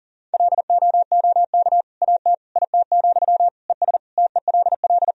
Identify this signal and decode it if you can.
Morse code, heard as a keyed tone, about 40 wpm, 700 Hz. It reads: LOOK AT IT, ES TELL
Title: Sentences from Top 200 Words and Abbreviations 40wpm